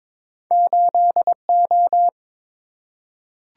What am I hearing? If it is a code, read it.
Morse code, heard as a keyed tone, about 22 wpm, 700 Hz. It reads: 8O
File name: practice / 22wpm / Two Letter-Number Combinations - 22wpm